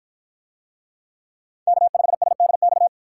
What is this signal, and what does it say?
Morse code, heard as a keyed tone, about 53 wpm, 700 Hz. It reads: K5SDX